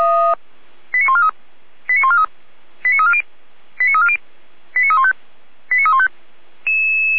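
Five-tone selective call (ZVEI), heard as quick runs of five tones, tones of about 70 ms. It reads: 89143 89143 89340 89340 89315 89315
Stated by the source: ZVEI five-tone call received by SDR.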